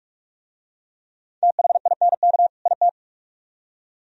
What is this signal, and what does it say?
Morse code, heard as a keyed tone, about 42 wpm, 700 Hz. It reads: THINK IT